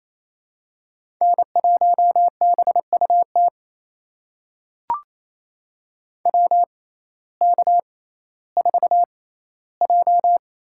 Morse code, heard as a keyed tone, about 28 wpm, spaced as for 15 wpm, 700 Hz. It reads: N1BUT WK4J
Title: USA Call Signs - 6x Character Spacing - 28wpm